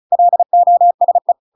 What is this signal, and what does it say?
Morse code, heard as a keyed tone, about 34 wpm, 700 Hz. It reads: LOSE